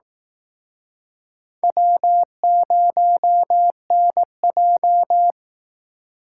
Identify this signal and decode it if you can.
Morse code, heard as a keyed tone, about 18 wpm, 700 Hz. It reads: W0NJ
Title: Speed Racing - USA 1x2 Call Signs 15wpm